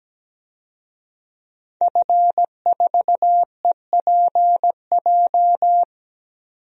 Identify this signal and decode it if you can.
Morse code, heard as a keyed tone, about 17 wpm, 700 Hz. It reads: F4EPJ
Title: DXCC-100 Call Signs - 5 Characters - 17wpm